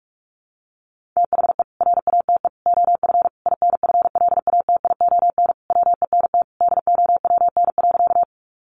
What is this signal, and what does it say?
Morse code, heard as a keyed tone, about 44 wpm, 700 Hz. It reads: THE RATE OF INFLATION WENT DOWN.
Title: Sentences from Top 1100 Words 45wpm